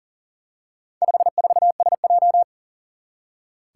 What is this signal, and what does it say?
Morse code, heard as a keyed tone, about 38 wpm, 700 Hz. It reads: 54SJ
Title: Four Challenging Letter-Number Combinations 40wpm